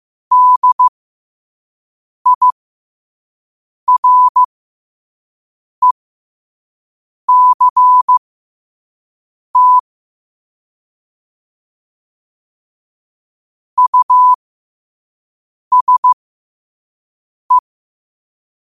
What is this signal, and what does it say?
Morse code, heard as a keyed tone, about 15 wpm, spaced as for 5 wpm, 1000 Hz. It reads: DIRECT USE
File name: qotdfiles / 5wpm